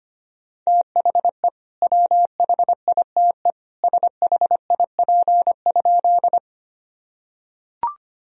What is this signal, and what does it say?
Morse code, heard as a keyed tone, about 25 wpm, 700 Hz. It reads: THE WHITE SHIP?
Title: Sentences from Top 1000 Words - Review Part 2 - 25wpm